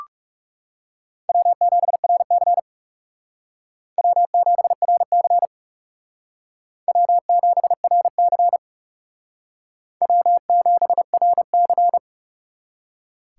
Morse code, heard as a keyed tone, about 37 wpm, 700 Hz. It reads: W7RC W7RC W7RC W7RC
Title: Speed Racing - USA 1x2 Call Signs 30wpm